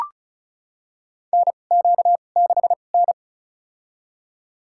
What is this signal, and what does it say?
Morse code, heard as a keyed tone, about 36 wpm, spaced as for 25 wpm, 700 Hz. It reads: NQ6N
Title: USA Call Signs - 4 Characters - 2x Character Spacing - 35wpm